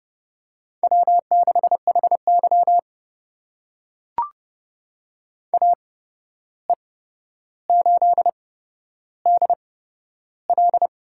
Morse code, heard as a keyed tone, about 29 wpm, spaced as for 12 wpm, 700 Hz. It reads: W6HY AE8DL
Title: USA Call Signs - 8x Character Spacing - 30wpm